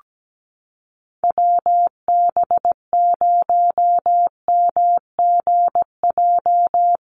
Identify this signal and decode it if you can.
Morse code, heard as a keyed tone, about 17 wpm, 700 Hz. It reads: WB0MGJ
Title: USA Call Signs Repeated 2x 17wpm